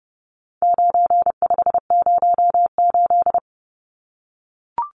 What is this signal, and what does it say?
Morse code, heard as a keyed tone, about 30 wpm, 700 Hz. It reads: 9508
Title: Numbers - 4-Digit 30wpm